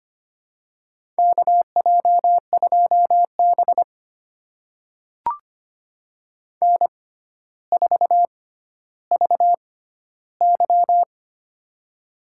Morse code, heard as a keyed tone, about 25 wpm, spaced as for 13 wpm, 700 Hz. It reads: KJ2B N4VY
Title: USA Call Signs - 4 Characters - 6x Character Spacing - 25wpm